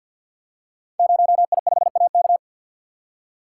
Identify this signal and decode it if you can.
Morse code, heard as a keyed tone, about 52 wpm, 700 Hz. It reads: 0I5AK
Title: Code Groups - 5 Letter:Numbers - 50wpm